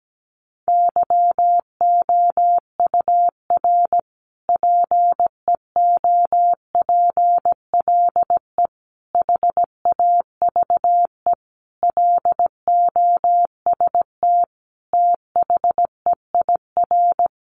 Morse code, heard as a keyed tone, about 17 wpm, 700 Hz. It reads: YOUR PEOPLE HAVE LOST THEIR
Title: Sentences from Top 1400 Words 17wpm